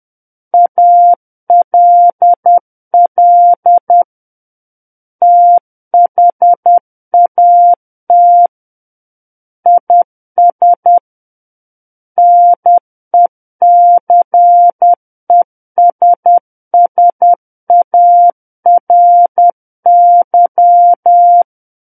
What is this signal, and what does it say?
Morse code, heard as a keyed tone, about 10 wpm, 700 Hz. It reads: ALL THAT IS NECESSARY
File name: Book One / 01-Chapter 1 - 10 WPM